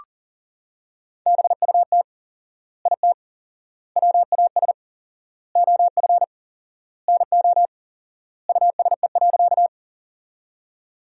Morse code, heard as a keyed tone, about 40 wpm, 700 Hz. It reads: BUT IT WAS OF NO USE.
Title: 4x Word-Spacing - Sentences from Top 100 Words - 40wpm